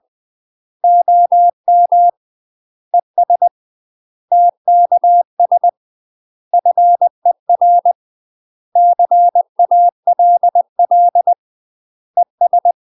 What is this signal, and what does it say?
Morse code, heard as a keyed tone, about 20 wpm, 700 Hz. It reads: OM ES TKS FER CALL ES
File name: UK MASTER QSOs TRAINING / 20WPM / UK MASTER QSOs 0043